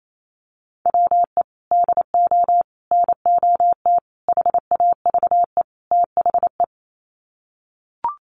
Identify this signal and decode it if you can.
Morse code, heard as a keyed tone, about 28 wpm, 700 Hz. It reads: WE DO NOT HAVE THE
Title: Sets of 5 Words 28wpm